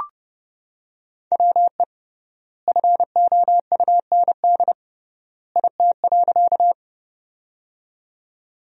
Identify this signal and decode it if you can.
Morse code, heard as a keyed tone, about 31 wpm, 700 Hz. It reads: WE FOUND IT.
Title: 3x Word-Spacing - Sentences from Top 300 Words - 30wpm